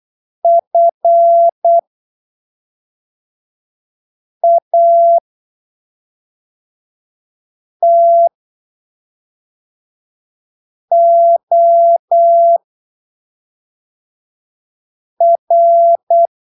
Morse code, heard as a keyed tone, about 8 wpm, spaced as for 3 wpm, 680 Hz. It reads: FATOR
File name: arquivos / texto / texto 09 08ppm